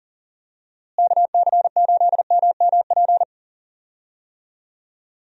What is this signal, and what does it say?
Morse code, heard as a keyed tone, about 40 wpm, 700 Hz. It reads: KC8MMP
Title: USA Call Signs - 6 Characters - Repeated 3x - 40wpm